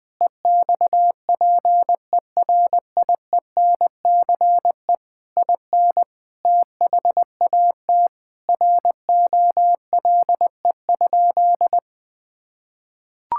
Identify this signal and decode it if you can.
Morse code, heard as a keyed tone, about 21 wpm, 700 Hz. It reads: EXPERIENCE IN THAT ROLE?
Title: Sentences from Top 1000 Words - Review Part 6 - 20wpm